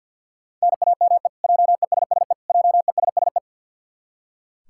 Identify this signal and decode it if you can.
Morse code, heard as a keyed tone, about 45 wpm, 700 Hz. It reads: NAME JESSE JESSE
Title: K1USN SST QSO Practice 50wpm